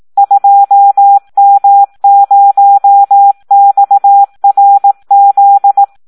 Morse code, heard as a keyed tone, about 18 wpm, 800 Hz. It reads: 2M0XRZ
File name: Audio practica CW / SD 1 18